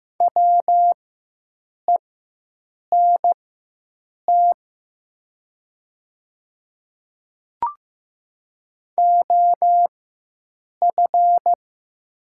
Morse code, heard as a keyed tone, about 15 wpm, spaced as for 7 wpm, 700 Hz. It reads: WENT OF
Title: 4x Character Spacing - Top 200 Words - 15wpm